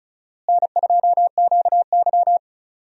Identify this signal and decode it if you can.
Morse code, heard as a keyed tone, about 36 wpm, 700 Hz. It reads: N2QY
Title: USA Call Signs - 4 Characters - Repeated 2x - 35wpm